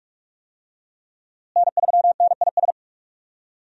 Morse code, heard as a keyed tone, about 46 wpm, 700 Hz. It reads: N3NIS
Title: USA Call Signs - 5 Characters - Repeated 3x - 45wpm